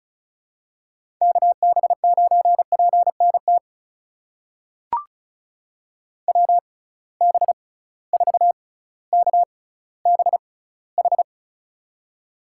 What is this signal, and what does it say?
Morse code, heard as a keyed tone, about 34 wpm, spaced as for 18 wpm, 700 Hz. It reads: KB9PNT WB4KBH
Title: USA Call Signs - 6 Characters - 6x Character Spacing - 35wpm